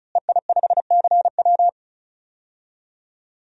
Morse code, heard as a keyed tone, about 36 wpm, 700 Hz. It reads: EI5CW
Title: DXCC-100 Call Signs - 5 Characters - Repeated 2x - 35wpm